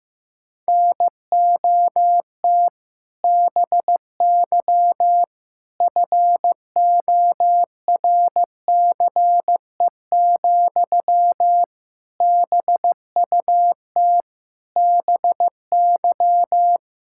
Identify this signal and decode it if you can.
Morse code, heard as a keyed tone, about 15 wpm, 700 Hz. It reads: NOT BY FORCE, BUT BY
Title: Sentences from Top 1000 Words - Review Part 5 - 15wpm